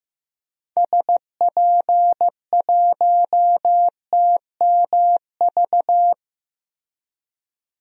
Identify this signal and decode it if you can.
Morse code, heard as a keyed tone, about 15 wpm, 700 Hz. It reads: SP1TMV
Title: DXCC-100 Call Signs - 6 Characters - Repeated 2x - 15wpm